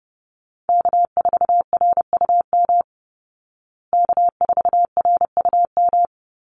Morse code, heard as a keyed tone, about 28 wpm, 700 Hz. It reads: K4RUM K4RUM
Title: Call Signs Repeated 3x - 30wpm